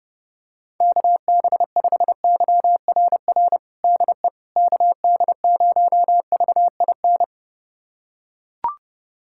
Morse code, heard as a keyed tone, about 30 wpm, 700 Hz. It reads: KB5YRR DE KD0VIN
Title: USA Callsign DE Callsign 30wpm